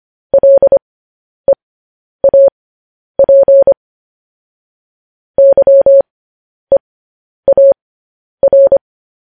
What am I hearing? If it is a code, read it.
Morse code, heard as a keyed tone, about 25 wpm, spaced as for 10 wpm, 550 Hz. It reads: LEAP YEAR